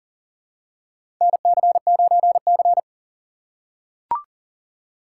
Morse code, heard as a keyed tone, about 38 wpm, 690 Hz. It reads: NC9C
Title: USA Call Signs - 4 Characters - 2x Character Spacing - 40wpm